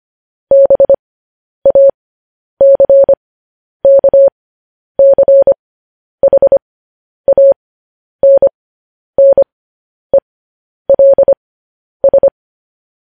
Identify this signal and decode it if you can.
Morse code, heard as a keyed tone, about 25 wpm, spaced as for 10 wpm, 550 Hz. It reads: BACKCHANNELS
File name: podcast / News headlines-1x-25x10